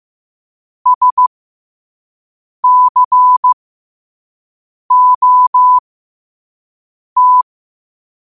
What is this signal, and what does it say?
Morse code, heard as a keyed tone, about 15 wpm, spaced as for 5 wpm, 1000 Hz. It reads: SCOT